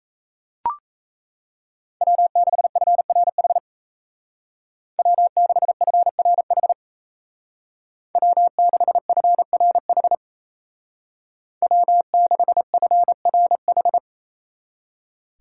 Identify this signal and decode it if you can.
Morse code, heard as a keyed tone, about 33 wpm, 700 Hz. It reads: W6FRH W6FRH W6FRH W6FRH
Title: Speed Racing - USA 1x3 Call Signs 28wpm